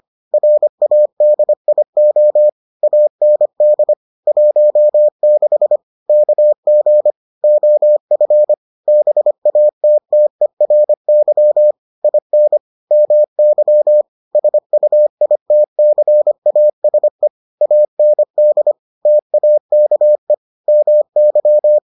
Morse code, heard as a keyed tone, about 25 wpm, 600 Hz. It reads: RADIO AND 16 KG OF BATTERY IN MY SUITCASE AND TAKE MY